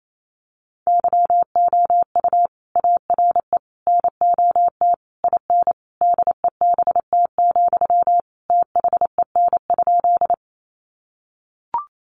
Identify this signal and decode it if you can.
Morse code, heard as a keyed tone, about 28 wpm, 700 Hz. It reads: YOU ARE NOT IN DEBT, THEN?
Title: Sentences from Top 1300 Words 28wpm